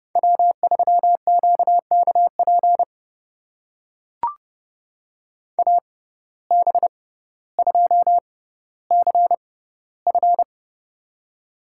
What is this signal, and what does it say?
Morse code, heard as a keyed tone, about 30 wpm, spaced as for 15 wpm, 700 Hz. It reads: W3QKP AB2CF
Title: USA Call Signs - 5 Characters - 6x Character Spacing - 30wpm